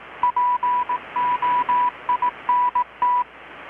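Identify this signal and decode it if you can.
Morse code, heard as a keyed tone, about 18 wpm, 980 Hz. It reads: POINT
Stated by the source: Morse from KPH's Night of Nights XX maritime radiotelegraphy event, received in Chiapas, Mexico about 3,800 km away on a Kenwood R-600 with a grounded 10 m long-wire antenna.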